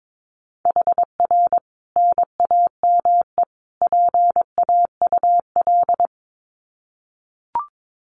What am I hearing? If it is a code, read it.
Morse code, heard as a keyed tone, about 22 wpm, 700 Hz. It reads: HR NAME PAUL
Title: Common QSO Phrases 22wpm